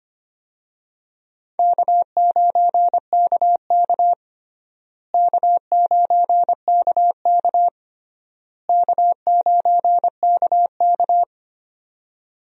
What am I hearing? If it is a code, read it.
Morse code, heard as a keyed tone, about 25 wpm, 700 Hz. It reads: K9KK K9KK K9KK
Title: USA Call Signs - 4 Characters - Repeated 3x - 25wpm